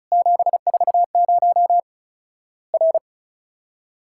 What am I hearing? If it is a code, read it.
Morse code, heard as a keyed tone, about 36 wpm, 700 Hz. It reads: 740 R
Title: Worked All Europe DX Contest - QTC Exchange - 35wpm